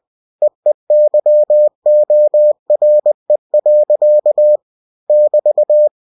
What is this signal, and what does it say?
Morse code, heard as a keyed tone, about 20 wpm, 600 Hz. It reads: EEYORE. <BT>
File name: cw-books / winnie-the-pooh / winnie-20wpm-page-0081